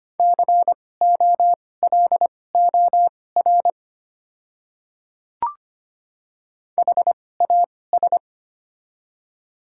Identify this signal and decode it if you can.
Morse code, heard as a keyed tone, about 25 wpm, spaced as for 18 wpm, 700 Hz. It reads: COLOR HAS
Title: 2x Character Spacing - Top 300 Words - 25wpm